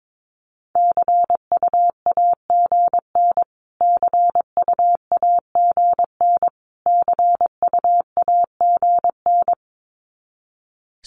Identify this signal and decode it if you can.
Morse code, heard as a keyed tone, about 22 wpm, 700 Hz. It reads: CUAGN CUAGN CUAGN
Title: Abbreviations Repeated 3x - 22wpm